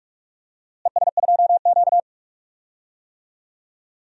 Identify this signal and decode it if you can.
Morse code, heard as a keyed tone, about 47 wpm, 700 Hz. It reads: ES2Q
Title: DXCC-100 Call Signs - 4 Characters - Repeated 2x - 45wpm